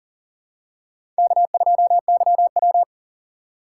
Morse code, heard as a keyed tone, about 39 wpm, 700 Hz. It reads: K2YW
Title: USA Call Signs - 4 Characters - 4x Character Spacing - 40wpm